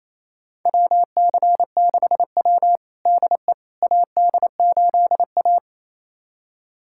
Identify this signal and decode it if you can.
Morse code, heard as a keyed tone, about 28 wpm, 700 Hz. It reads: WC6W DE AD8A
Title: USA Callsign DE Callsign - 4 Characters - 28wpm